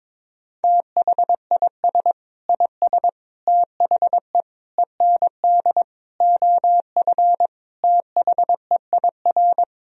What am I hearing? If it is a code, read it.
Morse code, heard as a keyed tone, about 21 wpm, 700 Hz. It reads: THIS IS THE END OF THEIR
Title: Sentences from Top 1000 Words - Review Part 1 - 22wpm